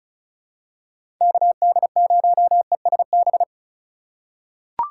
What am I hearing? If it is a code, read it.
Morse code, heard as a keyed tone, about 34 wpm, 700 Hz. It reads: KD0ESB
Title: USA Call Signs - 6 Characters - 4x Character Spacing - 35wpm